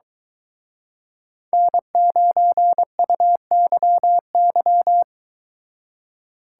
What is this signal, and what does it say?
Morse code, heard as a keyed tone, about 23 wpm, 700 Hz. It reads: N9UYY